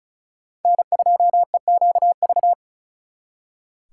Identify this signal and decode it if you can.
Morse code, heard as a keyed tone, about 44 wpm, 700 Hz. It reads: N2EQV
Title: Speed Racing - USA 1x3 Call Signs 35wpm